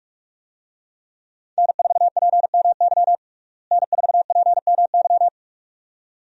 Morse code, heard as a keyed tone, about 41 wpm, 700 Hz. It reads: N4PMY N4PMY
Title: USA Call Signs - 5 Characters - Repeated 2x - 45wpm